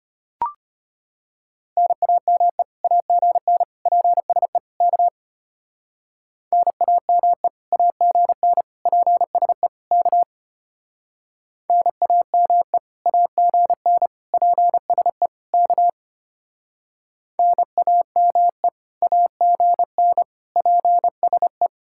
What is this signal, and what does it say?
Morse code, heard as a keyed tone, about 33 wpm, 700 Hz. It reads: NAME AGN PSE K NAME AGN PSE K NAME AGN PSE K NAME AGN PSE